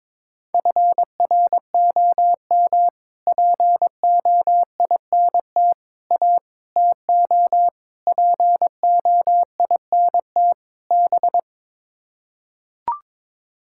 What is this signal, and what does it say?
Morse code, heard as a keyed tone, about 22 wpm, 700 Hz. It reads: FROM POINT A TO POINT B
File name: practice / 22wpm / Sets of 6 Words - 22wpm